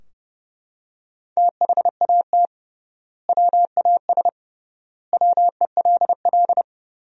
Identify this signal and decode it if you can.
Morse code, heard as a keyed tone, about 29 wpm, 700 Hz. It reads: THAT WAS WELL
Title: 3x Word-Spacing - Sentences from Top 500 Words - 30wpm